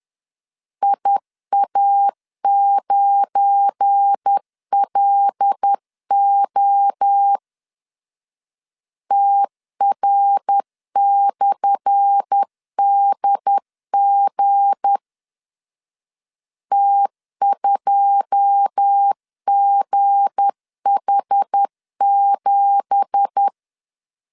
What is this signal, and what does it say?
Morse code, heard as a keyed tone, about 10 wpm, 800 Hz. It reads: IA9LO TR/DG T2GH7
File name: LEZIONI / 10 SIGNS / 10B INT 10wpm 2844